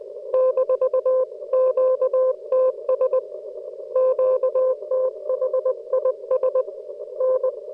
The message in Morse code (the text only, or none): -QTS QTHIS N